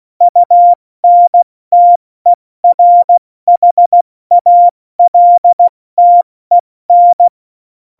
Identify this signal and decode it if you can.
Morse code, heard as a keyed tone, about 16 wpm, 700 Hz. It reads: UNTERHALTEN
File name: Morsekurs / Witze / Witz 01. Dezember 2020, 70 F80